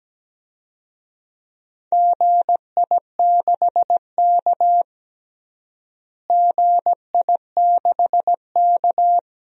Morse code, heard as a keyed tone, about 17 wpm, 700 Hz. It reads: GI6K GI6K
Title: DXCC-100 Call Signs - 4 Characters - Repeated 2x - 17wpm